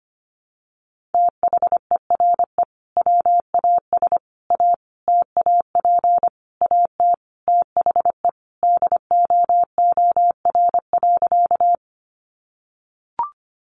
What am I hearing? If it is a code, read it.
Morse code, heard as a keyed tone, about 25 wpm, 700 Hz. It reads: THERE WAS A TAP AT THE DOOR.
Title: Sentences from Top 1400 Words 25wpm